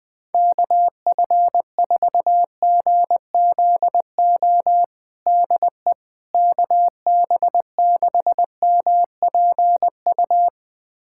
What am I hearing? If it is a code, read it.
Morse code, heard as a keyed tone, about 20 wpm, 700 Hz. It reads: KF4GZO DE KB6MPU
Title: USA Callsign DE Callsign - 6 Characters - 20wpm